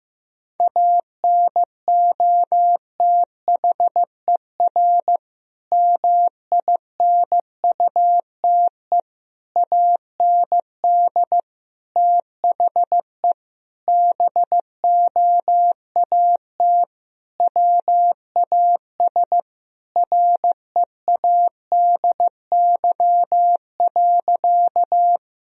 Morse code, heard as a keyed tone, about 15 wpm, 700 Hz. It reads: ANOTHER MINUTE AND THE BOAT WAS READY.
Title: Sentences from Top 1000 Words - Review Part 3 - 15wpm